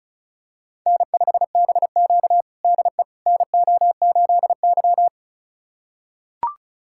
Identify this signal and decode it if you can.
Morse code, heard as a keyed tone, about 36 wpm, 700 Hz. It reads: N5BQ DE NO8Y
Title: USA Callsign DE Callsign - 4 Characters - 35wpm